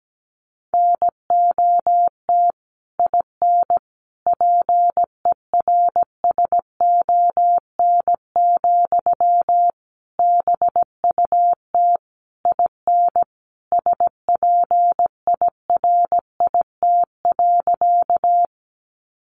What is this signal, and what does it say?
Morse code, heard as a keyed tone, about 17 wpm, 700 Hz. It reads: NOT IN PERSON, BUT IN SPIRIT.
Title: Sentences from Top 1400 Words 17wpm